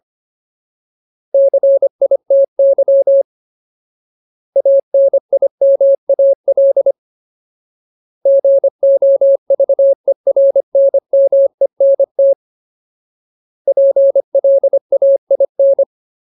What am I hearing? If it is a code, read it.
Morse code, heard as a keyed tone, about 25 wpm, 550 Hz. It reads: CITY ANIMAL GOVERNMENT PLAIN
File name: Ultimate Word List / 25WPM Ultimate